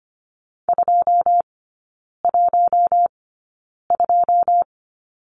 Morse code, heard as a keyed tone, about 25 wpm, spaced as for 9 wpm, 690 Hz. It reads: 212
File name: Lesson 15e - Number 5 - Three Characters 25wpm